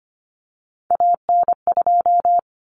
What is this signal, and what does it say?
Morse code, heard as a keyed tone, about 25 wpm, 700 Hz. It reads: AN2
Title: Common Cut Numbers - 3-Digit 25wpm